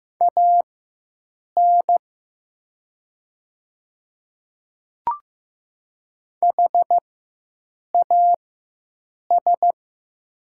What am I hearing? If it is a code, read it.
Morse code, heard as a keyed tone, about 15 wpm, spaced as for 7 wpm, 700 Hz. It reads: AN HAS
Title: 4x Character Spacing - Top 100 Words - 15wpm